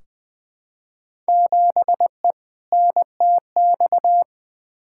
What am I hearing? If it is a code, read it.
Morse code, heard as a keyed tone, about 20 wpm, 700 Hz. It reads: 7E NTX
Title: Field Day Exchanges Encore - 20wpm